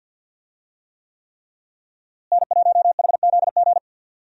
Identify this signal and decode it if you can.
Morse code, heard as a keyed tone, about 51 wpm, 700 Hz. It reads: N1HZG
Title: USA Call Signs - 5 Characters - Repeated 3x - 50wpm